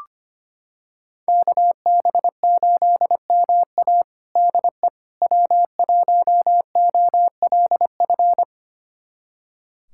Morse code, heard as a keyed tone, about 25 wpm, 700 Hz. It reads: KB8MA DE W1OLF
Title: USA Callsign DE Callsign - 5 Characters - 25wpm